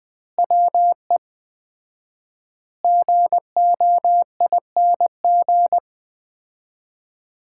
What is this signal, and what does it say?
Morse code, heard as a keyed tone, about 20 wpm, 700 Hz. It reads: WE GOING
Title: 4x Word-Spacing - Sets of 4 Words - 20wpm